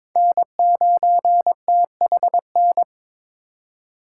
Morse code, heard as a keyed tone, about 26 wpm, 700 Hz. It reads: N9THN
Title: Speed Racing - USA 1x3 Call Signs 22wpm